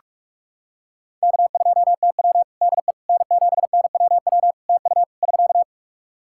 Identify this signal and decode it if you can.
Morse code, heard as a keyed tone, about 46 wpm, 700 Hz. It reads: K2TW DE N7NWW TU <SK>
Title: K1USN SST QSO Practice 45wpm